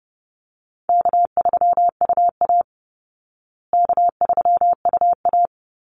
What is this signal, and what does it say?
Morse code, heard as a keyed tone, about 30 wpm, 700 Hz. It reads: K3UA K3UA
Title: Easy Call Signs Repeated 3x - 30wpm